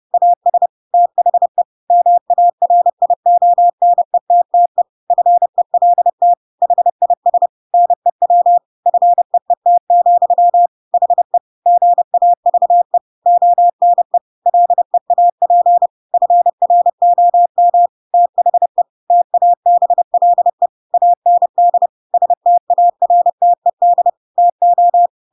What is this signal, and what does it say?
Morse code, heard as a keyed tone, about 30 wpm, 700 Hz. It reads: AS THE MARIONETTE FELT HIS NEW FEET, HE GAVE ONE LEAP FROM THE TABLE AND STARTED TO
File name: Pinocchio / Pinocchio ch8 30 30 ik1hge